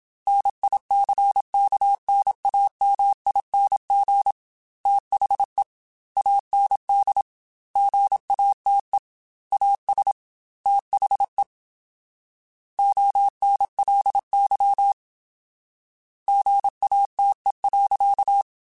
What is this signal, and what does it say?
Morse code, heard as a keyed tone, about 27 wpm, 800 Hz. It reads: NICKNAMING THE AND GATE AS THE ONLY GATE.
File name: Technical Subjects / Technical Subjects-mp3s / AND Gate @25